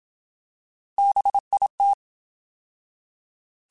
Morse code, heard as a keyed tone, about 27 wpm, 800 Hz. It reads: BIT
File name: Alphabetical / Alphabetical mp3s / Bs @25